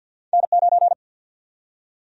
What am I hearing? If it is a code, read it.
Morse code, heard as a keyed tone, about 50 wpm, 700 Hz. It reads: N9
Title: USA Call Sign Prefixes - 50wpm